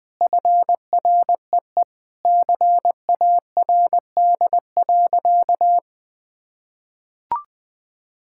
Morse code, heard as a keyed tone, about 20 wpm, 700 Hz. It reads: FREE CARD.
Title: Sentences with 7-Word Phrases - Spoken Beginning - 20wpm